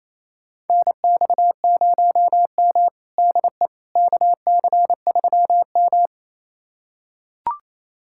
Morse code, heard as a keyed tone, about 28 wpm, 700 Hz. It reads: NX0M DE KC3M
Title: USA Callsign DE Callsign - 4 Characters - 28wpm